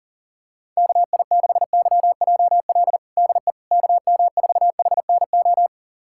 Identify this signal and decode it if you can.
Morse code, heard as a keyed tone, about 37 wpm, 700 Hz. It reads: KI6YJL DE KM4HNO
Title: USA Callsign DE Callsign - 6 Characters - 40wpm